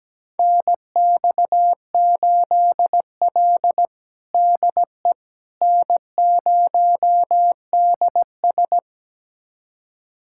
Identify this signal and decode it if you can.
Morse code, heard as a keyed tone, about 17 wpm, 700 Hz. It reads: NX8L DE N0DS
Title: USA Callsign DE Callsign - 4 Characters - 17wpm